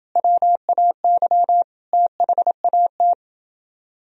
Morse code, heard as a keyed tone, about 28 wpm, 700 Hz. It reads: WAY THAT